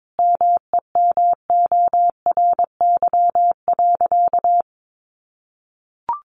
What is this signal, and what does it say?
Morse code, heard as a keyed tone, about 22 wpm, 700 Hz. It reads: MEMORY.
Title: Sentences from Top 1300 Words 22wpm